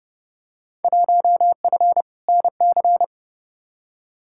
Morse code, heard as a keyed tone, about 32 wpm, 700 Hz. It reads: 1F NC
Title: Field Day Exchanges - 30wpm